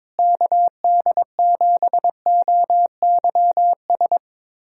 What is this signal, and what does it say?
Morse code, heard as a keyed tone, about 22 wpm, 700 Hz. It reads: KD7OYS
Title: USA Callsign DE Callsign - 6 Characters - 22wpm